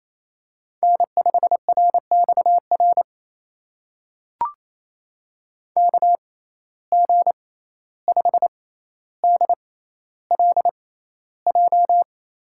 Morse code, heard as a keyed tone, about 28 wpm, spaced as for 14 wpm, 700 Hz. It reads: N5RXR KG5DLJ
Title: USA Call Signs - 6x Character Spacing - 28wpm